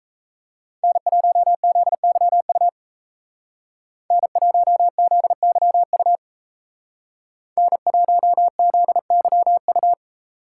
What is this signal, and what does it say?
Morse code, heard as a keyed tone, about 35 wpm, 700 Hz. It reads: N1ZYU N1ZYU N1ZYU
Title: Speed Racing - USA 1x3 Call Signs 28wpm